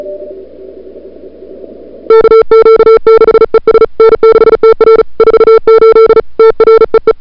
Morse code, heard as a keyed tone, about 36 wpm, 440 Hz. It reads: KQ6ES N6TR 48 TREE
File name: n6tr / Fab2007CWSprintHour1